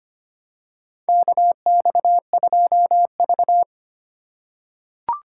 Morse code, heard as a keyed tone, about 25 wpm, 700 Hz. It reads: KX2V
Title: USA Call Signs - 4 Characters - 4x Character Spacing - 25wpm